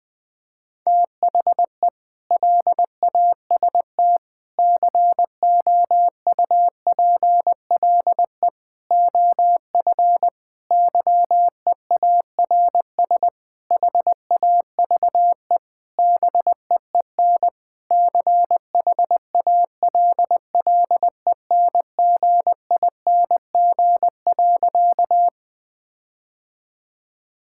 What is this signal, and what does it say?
Morse code, heard as a keyed tone, about 19 wpm, 700 Hz. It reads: THE LAST COUPLE OF YEARS HAVE BEEN CHALLENGING.
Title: Sentences with 5-Word Phrases - 20wpm